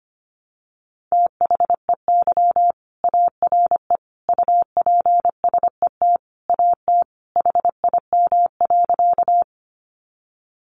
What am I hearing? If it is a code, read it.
Morse code, heard as a keyed tone, about 25 wpm, 700 Hz. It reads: THEY ARE UPSET AT HIM.
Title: Sentences from Top 800 Words 25wpm